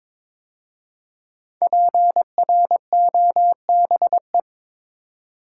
Morse code, heard as a keyed tone, about 23 wpm, 700 Hz. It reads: PROBE
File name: 5 Letter Words - First 3 Letters Given - 22wpm